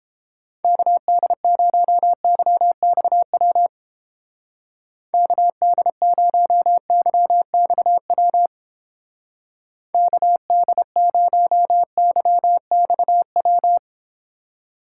Morse code, heard as a keyed tone, about 29 wpm, 700 Hz. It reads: KD0YXW KD0YXW KD0YXW